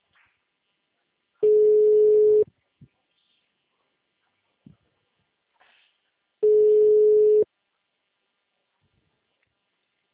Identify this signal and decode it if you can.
Ringback tone (Europe, 425 Hz).